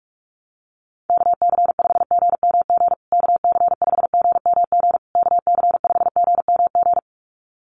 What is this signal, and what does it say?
Morse code, heard as a keyed tone, about 44 wpm, 690 Hz. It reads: KC5GMG KC5GMG KC5GMG
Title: USA Call Signs Repeated 3x 45wpm